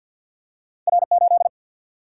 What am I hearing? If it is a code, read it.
Morse code, heard as a keyed tone, about 52 wpm, 700 Hz. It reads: R8
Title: Single-Letter Single-Number 50wpm